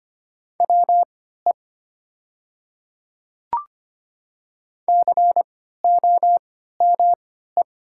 Morse code, heard as a keyed tone, about 25 wpm, spaced as for 14 wpm, 700 Hz. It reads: WE COME
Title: 3x Character Spacing - Top 100 Words - 25wpm